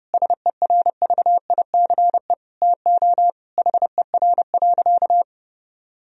Morse code, heard as a keyed tone, about 32 wpm, 700 Hz. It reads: SERVICE TO HER.
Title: Sentences from Top 1000 Words - Review Part 4 - 30wpm